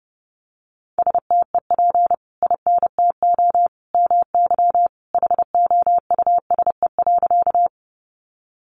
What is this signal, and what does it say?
Morse code, heard as a keyed tone, about 30 wpm, 700 Hz. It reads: STEP INTO MY HOUSE.
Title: Sentences from Top 600 Words 30wpm